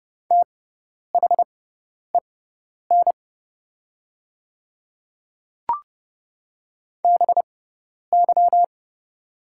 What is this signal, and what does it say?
Morse code, heard as a keyed tone, about 30 wpm, spaced as for 10 wpm, 700 Hz. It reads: THEN BY
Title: 6x Character Spacing - Top 100 Words - 30wpm